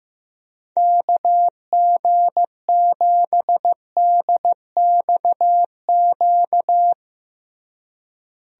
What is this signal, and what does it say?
Morse code, heard as a keyed tone, about 15 wpm, 700 Hz. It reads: KG7DXQ
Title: USA Call Signs - 6 Characters - Repeated 2x - 15wpm